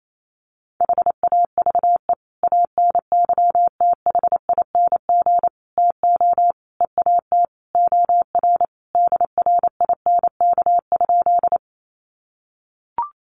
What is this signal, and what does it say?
Morse code, heard as a keyed tone, about 28 wpm, 700 Hz. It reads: HAVE ANYTHING TO EAT OR DRINK?
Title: Sentences from Top 1500 Words 28wpm